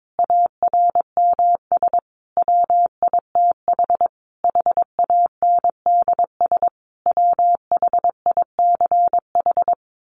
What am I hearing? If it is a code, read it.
Morse code, heard as a keyed tone, about 22 wpm, 700 Hz. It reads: ARMS WITH HANDS WHICH
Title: A Sound Of Thunder 22wpm